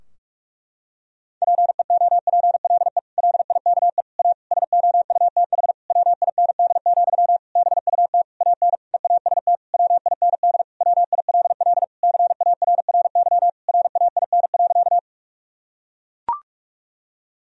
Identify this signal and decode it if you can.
Morse code, heard as a keyed tone, about 41 wpm, 700 Hz. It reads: PEOPLE LIKE A SOUTH WIND, BUT AN EAST WIND WILL CARRY RAIN.
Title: Sentences from Top 1000 Words - Review Part 3 - 45wpm